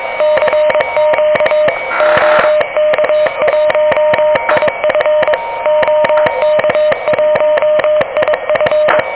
Morse code, heard as a keyed tone, about 21 wpm, 600 Hz, with noise. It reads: CQ OK1IF OK1IF
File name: OK QRP 2023 / OK QRP 2023